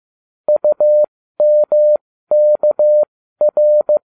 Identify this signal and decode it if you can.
Morse code, heard as a keyed tone, about 15 wpm, spaced as for 13 wpm, 600 Hz. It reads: UMKR